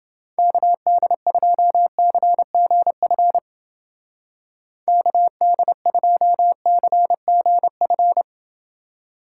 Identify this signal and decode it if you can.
Morse code, heard as a keyed tone, about 29 wpm, 700 Hz. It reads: KD2CGF KD2CGF